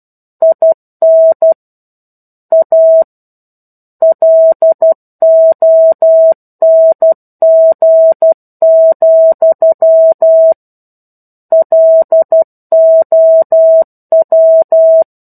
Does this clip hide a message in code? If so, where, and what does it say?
Yes: Morse code from 0.4 to 15.0 s, heard as a keyed tone, about 12 wpm, 650 Hz. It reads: IN A LONG, LOW